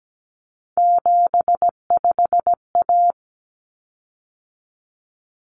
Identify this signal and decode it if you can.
Morse code, heard as a keyed tone, about 17 wpm, 700 Hz. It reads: 75A
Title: Common Cut Numbers - 3-Digit 17wpm